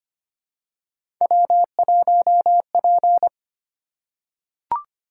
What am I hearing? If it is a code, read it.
Morse code, heard as a keyed tone, about 25 wpm, 700 Hz. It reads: W1P
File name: Three Letter-Number Combinations - 25wpm